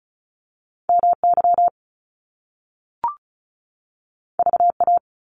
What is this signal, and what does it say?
Morse code, heard as a keyed tone, about 36 wpm, 700 Hz. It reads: MY VA
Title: Top Two Letter Combinations in Words 35wpm